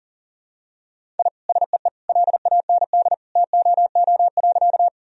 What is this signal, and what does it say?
Morse code, heard as a keyed tone, about 40 wpm, 700 Hz. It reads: I SEE LAND TOO.
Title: Sentences from Top 1000 Words - Review Part 1 - 40wpm